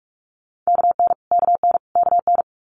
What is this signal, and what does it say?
Morse code, heard as a keyed tone, about 46 wpm, 700 Hz. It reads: KN KN KN
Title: Multi-letter Phonemes Repeated 3x 45wpm